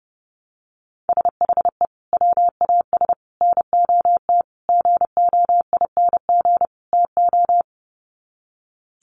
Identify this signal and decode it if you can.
Morse code, heard as a keyed tone, about 30 wpm, 700 Hz. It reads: SHE WAS NOT GOING TO
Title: Sets of 5 Words Encore2 30wpm